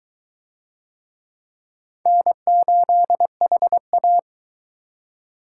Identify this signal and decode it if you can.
Morse code, heard as a keyed tone, about 19 wpm, 700 Hz. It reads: N8HA